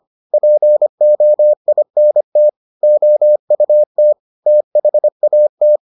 Morse code, heard as a keyed tone, about 25 wpm, 600 Hz. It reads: POINT OUT THAT